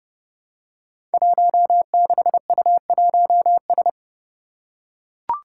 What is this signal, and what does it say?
Morse code, heard as a keyed tone, about 31 wpm, 700 Hz. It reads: 16U1S
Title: Five Challenging Letter-Number Combinations 30wpm